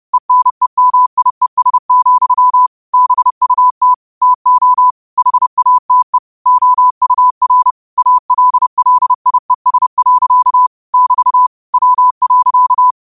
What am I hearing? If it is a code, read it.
Morse code, heard as a keyed tone, about 30 wpm, 1000 Hz. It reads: ENEMIES, BUT TO HATE OUR ALLIES. <BT> W.
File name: 30wpm